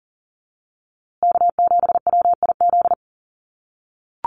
Morse code, heard as a keyed tone, about 40 wpm, 700 Hz. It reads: K7WIZ
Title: USA Call Signs Repeated 2x 40wpm